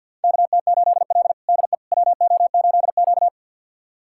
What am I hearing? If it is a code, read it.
Morse code, heard as a keyed tone, about 50 wpm, 700 Hz. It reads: KT8L DE WO8Q
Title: USA Callsign DE Callsign - 4 Characters - 50wpm